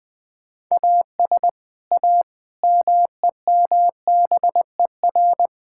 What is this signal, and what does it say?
Morse code, heard as a keyed tone, about 21 wpm, 700 Hz. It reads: AS A MEMBER
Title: Sentences with 5-Word Phrases - Spoken Beginning - 20wpm